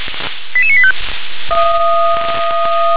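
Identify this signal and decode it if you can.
Five-tone selective call (ZVEI), heard as quick runs of five tones, tones of about 70 ms. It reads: 88985
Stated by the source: Five-tone ZVEI selective call received on SDR.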